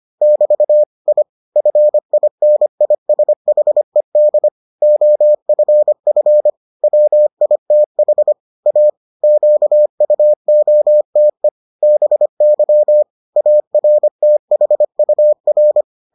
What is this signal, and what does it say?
Morse code, heard as a keyed tone, about 25 wpm, 600 Hz. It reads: <BT> I FINISHED OFF WITH A QUOTE BY ARTHUR